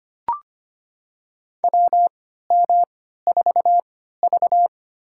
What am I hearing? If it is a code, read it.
Morse code, heard as a keyed tone, about 25 wpm, spaced as for 14 wpm, 700 Hz. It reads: WM4V
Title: USA Call Signs - 4 Characters - 3x Character Spacing - 25wpm